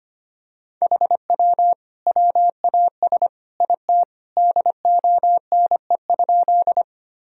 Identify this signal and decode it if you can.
Morse code, heard as a keyed tone, about 25 wpm, 700 Hz. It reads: HW WAS IT DONE?
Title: Sentences from Top 500 Words and Abbreviations 25wpm